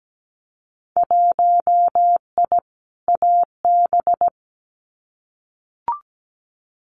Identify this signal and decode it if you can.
Morse code, heard as a keyed tone, about 17 wpm, 700 Hz. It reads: 1I AB
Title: Winter Field Day Exchanges Encore 17wpm